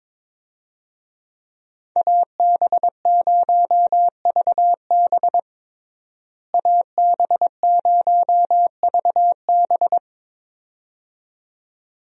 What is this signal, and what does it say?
Morse code, heard as a keyed tone, about 22 wpm, 700 Hz. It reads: AB0VB AB0VB
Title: USA Call Signs - 5 Characters - Repeated 2x - 22wpm